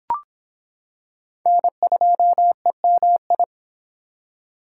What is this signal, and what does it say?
Morse code, heard as a keyed tone, about 25 wpm, 700 Hz. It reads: N2EMI